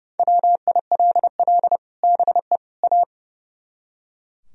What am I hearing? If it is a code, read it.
Morse code, heard as a keyed tone, about 40 wpm, spaced as for 35 wpm, 700 Hz. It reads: WILL BE A
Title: Speed-Racing - Sets of 3 Words 30wpm